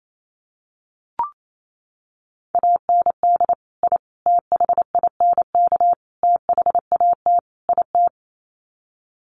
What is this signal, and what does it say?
Morse code, heard as a keyed tone, about 28 wpm, 700 Hz. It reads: AND I THINK THAT IT
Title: Sets of 5 Words Encore2 28wpm